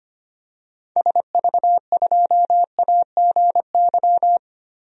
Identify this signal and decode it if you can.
Morse code, heard as a keyed tone, about 25 wpm, 700 Hz. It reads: SV2AGY
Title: DXCC-100 Call Signs - 6 Characters - Repeated 2x - 25wpm